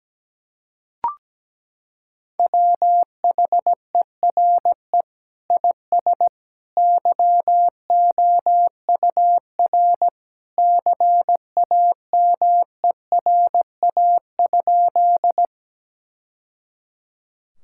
Morse code, heard as a keyed tone, about 17 wpm, 700 Hz. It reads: WHERE IS YOUR CAMERA?
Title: Sentences from Top 1000 Words - Review Part 5 - 17wpm